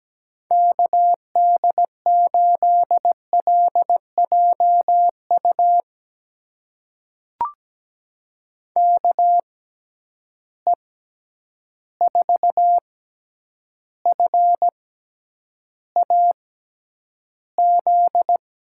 Morse code, heard as a keyed tone, about 17 wpm, spaced as for 9 wpm, 700 Hz. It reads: KD8LJU KE4FAZ
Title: USA Call Signs - 6 Characters - 6x Character Spacing - 17wpm